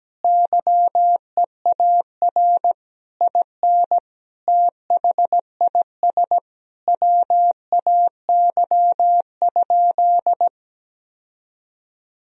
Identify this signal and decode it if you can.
Morse code, heard as a keyed tone, about 17 wpm, 700 Hz. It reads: YEAR IN THIS WAY?
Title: Sentences from Top 1000 Words - Review Part 1 - 17wpm